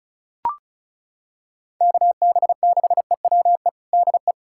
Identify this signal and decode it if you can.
Morse code, heard as a keyed tone, about 34 wpm, 700 Hz. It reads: KB6EWE DE
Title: USA Callsign DE Callsign - 6 Characters - 35wpm